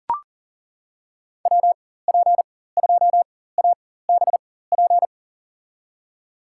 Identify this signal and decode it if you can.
Morse code, heard as a keyed tone, about 40 wpm, spaced as for 19 wpm, 700 Hz. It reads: WP2ABP
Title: USA Call Signs - 6 Characters - 4x Character Spacing - 40wpm